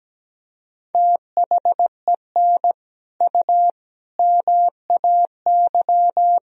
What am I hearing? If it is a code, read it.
Morse code, heard as a keyed tone, about 17 wpm, 700 Hz. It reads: THEN U MAY
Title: Sentences from Top 100 Words and Abbreviations 17wpm